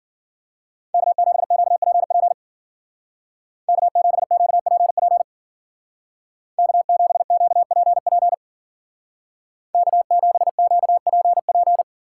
Morse code, heard as a keyed tone, about 51 wpm, 700 Hz. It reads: K7QPP K7QPP K7QPP K7QPP